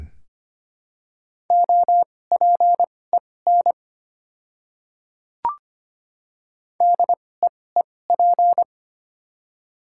Morse code, heard as a keyed tone, about 25 wpm, spaced as for 18 wpm, 700 Hz. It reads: OPEN DEEP
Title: 2x Character Spacing - Top 500 Words - 25wpm